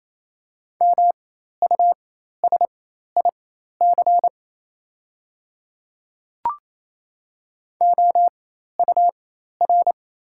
Morse code, heard as a keyed tone, about 28 wpm, spaced as for 13 wpm, 690 Hz. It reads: MUSIC OUR
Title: 4x Character Spacing - Top 300 Words - 28wpm